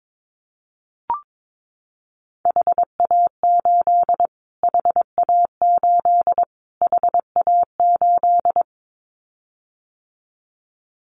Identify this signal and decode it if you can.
Morse code, heard as a keyed tone, about 22 wpm, 700 Hz. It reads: HA8 HA8 HA8
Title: Callsign Prefixes for Top 25 DXCC - Repeated 3x - 22wpm